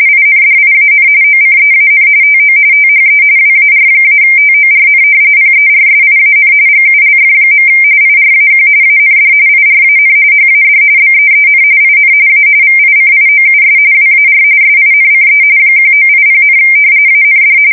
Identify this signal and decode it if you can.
RTTY, about 45.45 baud. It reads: INDIAN OCEAN TO THE WAR-TORN PERSIAN GULF. IRAQ AND IRAN HAD BEEN IN A VICIOUS WAR SINCE 1980, AND
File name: fsk170 / battleshipnj